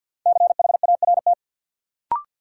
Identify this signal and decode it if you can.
Morse code, heard as a keyed tone, about 53 wpm, 700 Hz. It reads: CHART
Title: Sentences from Top 1300 Words - New Word Drill - 50wpm